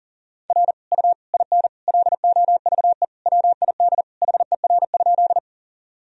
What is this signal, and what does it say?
Morse code, heard as a keyed tone, about 40 wpm, 700 Hz. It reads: R U IN LOVE WID HER?
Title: Sentences from Top 400 Words and Abbreviations 40wpm